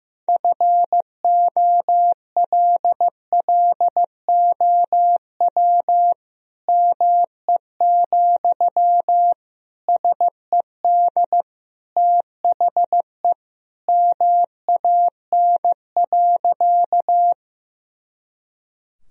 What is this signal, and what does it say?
Morse code, heard as a keyed tone, about 15 wpm, 700 Hz. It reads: FOLLOW ME, SED THE MAN.
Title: Sentences from Top 200 Words and Abbreviations 15wpm